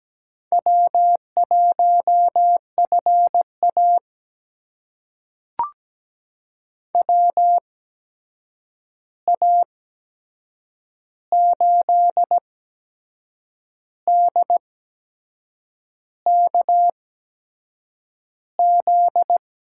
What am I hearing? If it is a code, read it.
Morse code, heard as a keyed tone, about 17 wpm, spaced as for 6 wpm, 700 Hz. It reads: W1FA WA8DKZ
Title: USA Call Signs - 8x Character Spacing - 17wpm